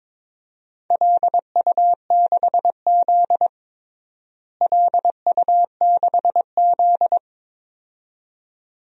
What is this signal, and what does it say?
Morse code, heard as a keyed tone, about 22 wpm, 700 Hz. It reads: LU6Z LU6Z
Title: DXCC-100 Call Signs - 4 Characters - Repeated 2x - 22wpm